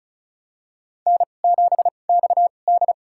Morse code, heard as a keyed tone, about 36 wpm, spaced as for 25 wpm, 700 Hz. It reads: N7XD
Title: USA Call Signs - 4 Characters - 2x Character Spacing - 35wpm